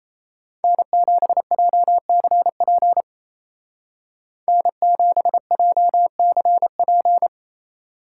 Morse code, heard as a keyed tone, about 33 wpm, 700 Hz. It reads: N7JCP N7JCP
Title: Speed Racing - USA 1x3 Call Signs 28wpm